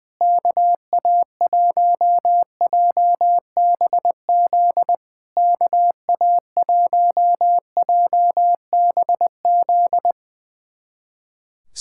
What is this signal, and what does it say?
Morse code, heard as a keyed tone, about 20 wpm, 690 Hz. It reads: KA1JBZ KA1JBZ
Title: SOTA QSO Practice 20wpm